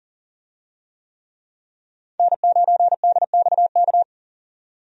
Morse code, heard as a keyed tone, about 41 wpm, 700 Hz. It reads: N9DXK
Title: USA Call Signs - 5 Characters - 40wpm